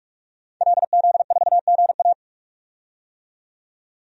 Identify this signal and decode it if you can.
Morse code, heard as a keyed tone, about 41 wpm, 700 Hz. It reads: LZ4GA
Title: DXCC-100 Call Signs - 5 Characters - 45wpm